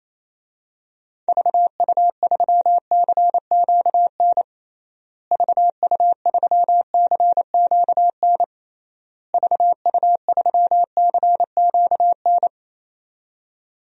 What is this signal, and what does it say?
Morse code, heard as a keyed tone, about 28 wpm, 700 Hz. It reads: VU3CQN VU3CQN VU3CQN
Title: DXCC-100 Call Signs - 6 Characters - Repeated 3x - 28wpm